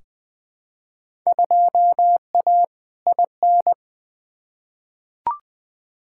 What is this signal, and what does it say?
Morse code, heard as a keyed tone, about 20 wpm, 700 Hz. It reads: 2A IN
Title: Field Day Exchanges - 20wpm